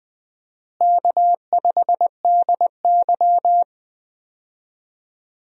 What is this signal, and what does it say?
Morse code, heard as a keyed tone, about 20 wpm, 700 Hz. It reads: K5DY
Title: USA Call Signs - 4 Characters - Repeated 3x - 20wpm